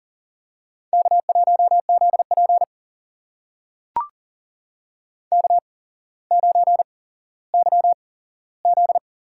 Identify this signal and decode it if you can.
Morse code, heard as a keyed tone, about 40 wpm, spaced as for 17 wpm, 700 Hz. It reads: K1ZP K9YZ
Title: USA Call Signs - 4 Characters - 8x Character Spacing - 40wpm